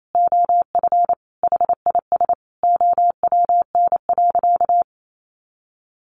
Morse code, heard as a keyed tone, about 28 wpm, 700 Hz. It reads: OF HIS OWN.
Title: Sentences from Top 300 Words 28wpm